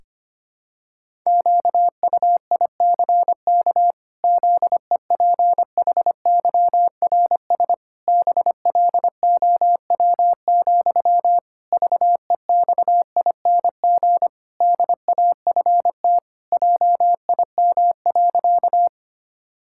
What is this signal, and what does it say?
Morse code, heard as a keyed tone, about 25 wpm, 700 Hz. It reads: QUICK ZEPHYRS BLOW, VEXING DAFT JIM.
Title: Pangram Challenge v2 - 25wpm